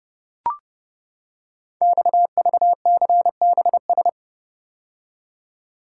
Morse code, heard as a keyed tone, about 30 wpm, 700 Hz. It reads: XVCBS
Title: Code Groups - 5 Letters - 30wpm